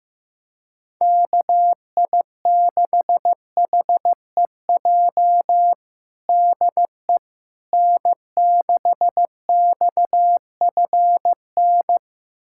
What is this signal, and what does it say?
Morse code, heard as a keyed tone, about 15 wpm, 700 Hz. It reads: KI6HEJ DE N6XFN
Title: USA Callsign DE Callsign 15wpm